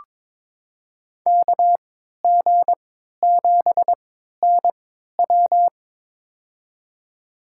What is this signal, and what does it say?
Morse code, heard as a keyed tone, about 22 wpm, spaced as for 12 wpm, 700 Hz. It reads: KG7NW
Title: USA Call Signs - 3x Character Spacing - 22wpm